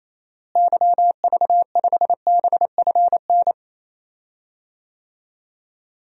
Morse code, heard as a keyed tone, about 28 wpm, 700 Hz. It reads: YV5BFN
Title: DXCC-100 Call Signs - 6 Characters - Repeated 2x - 28wpm